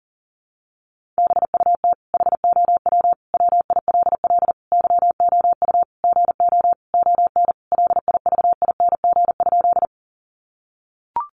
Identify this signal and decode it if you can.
Morse code, heard as a keyed tone, about 37 wpm, 700 Hz. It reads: BUT HOW WILL YOU GO ON LIVING?
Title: Sentences from Top 1100 Words 40wpm